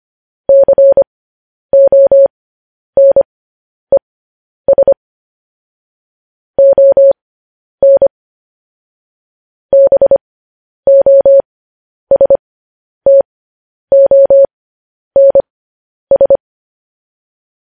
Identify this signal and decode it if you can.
Morse code, heard as a keyed tone, about 25 wpm, spaced as for 10 wpm, 550 Hz. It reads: CONES ON BOSTONS
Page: news-headlines-in-morse-code-at-10-wpm